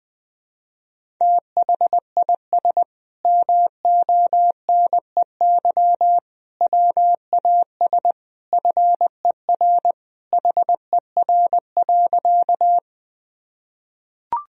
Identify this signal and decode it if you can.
Morse code, heard as a keyed tone, about 20 wpm, 700 Hz. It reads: THIS MONEY WAS FER HER.
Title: Sentences from Top 400 Words and Abbreviations 20wpm